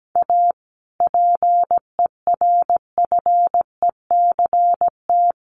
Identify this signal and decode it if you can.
Morse code, heard as a keyed tone, about 17 wpm, 700 Hz. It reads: A PERFECT
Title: Sentences from Top 1000 Words 17wpm